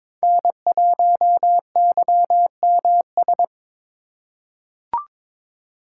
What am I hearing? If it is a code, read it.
Morse code, heard as a keyed tone, about 22 wpm, 700 Hz. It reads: N1YMS
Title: USA Call Signs - 5 Characters - 2x Character Spacing - 22wpm